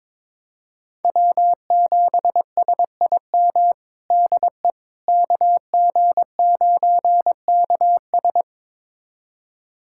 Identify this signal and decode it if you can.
Morse code, heard as a keyed tone, about 22 wpm, 700 Hz. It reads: W7SIM DE KG9KS
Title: USA Callsign DE Callsign - 5 Characters - 22wpm